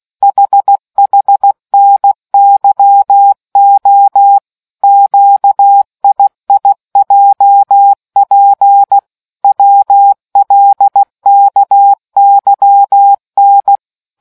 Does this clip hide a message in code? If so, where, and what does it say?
Yes: Morse code from 0.2 to 13.8 s, heard as a keyed tone, about 16 wpm, 800 Hz. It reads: HHNYO QIIJP WLKYN